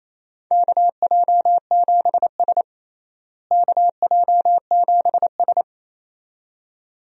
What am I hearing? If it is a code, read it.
Morse code, heard as a keyed tone, about 28 wpm, 700 Hz. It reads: KJ7S KJ7S
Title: USA Call Signs - 4 Characters - Repeated 2x - 28wpm